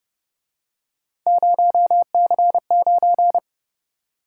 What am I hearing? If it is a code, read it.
Morse code, heard as a keyed tone, about 32 wpm, 700 Hz. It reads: 0C9